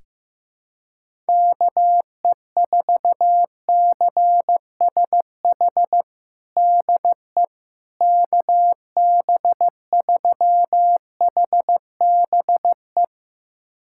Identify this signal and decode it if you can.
Morse code, heard as a keyed tone, about 15 wpm, 700 Hz. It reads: KE4CSH DE KB3HBE
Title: USA Callsign DE Callsign 15wpm